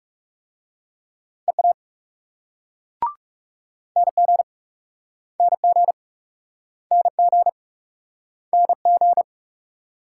Morse code, heard as a keyed tone, about 42 wpm, 700 Hz. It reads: EA NG NG NG NG